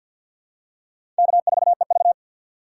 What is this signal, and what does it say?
Morse code, heard as a keyed tone, about 52 wpm, 700 Hz. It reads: K4EV
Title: USA Call Signs - 4 Characters - 2x Character Spacing - 50wpm